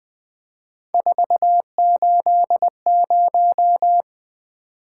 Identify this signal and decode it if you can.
Morse code, heard as a keyed tone, about 20 wpm, 700 Hz. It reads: 480
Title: Three Letter-Number Combinations - 20wpm